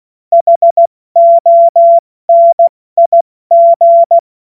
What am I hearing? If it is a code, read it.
Morse code, heard as a keyed tone, about 16 wpm, 680 Hz. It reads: HONIG
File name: Morsekurs / Witze / Witz 07. Februar 2023 70 F80